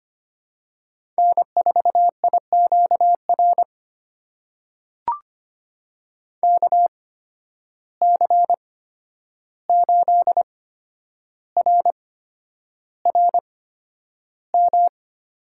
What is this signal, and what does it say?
Morse code, heard as a keyed tone, about 25 wpm, spaced as for 10 wpm, 700 Hz. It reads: N4IQR KC8RRM
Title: USA Call Signs - 8x Character Spacing - 25wpm